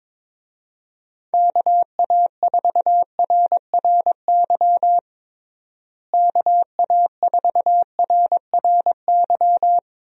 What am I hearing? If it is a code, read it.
Morse code, heard as a keyed tone, about 22 wpm, 700 Hz. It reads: KA4RRY KA4RRY
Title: USA Call Signs - 6 Characters - Repeated 2x - 22wpm